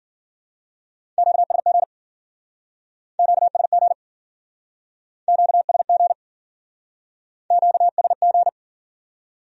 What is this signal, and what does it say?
Morse code, heard as a keyed tone, about 52 wpm, 700 Hz. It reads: QSG QSG QSG QSG